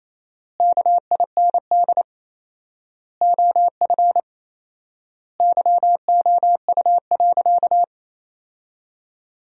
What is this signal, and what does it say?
Morse code, heard as a keyed tone, about 28 wpm, 700 Hz. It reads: KIND OF YOU.
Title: 4x Word-Spacing - Sentences from Top 200 Words - 28wpm